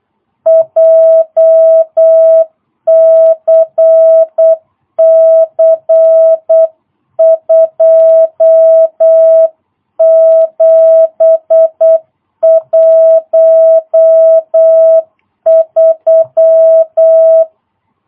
Morse code, heard as a keyed tone, about 8 wpm, 650 Hz. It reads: JCC 2713
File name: jh8cbh / 2022cw